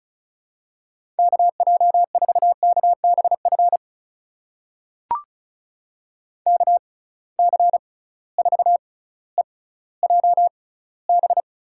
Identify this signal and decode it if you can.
Morse code, heard as a keyed tone, about 36 wpm, spaced as for 18 wpm, 700 Hz. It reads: KJ4KBF KC4EJB
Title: USA Call Signs - 6 Characters - 6x Character Spacing - 35wpm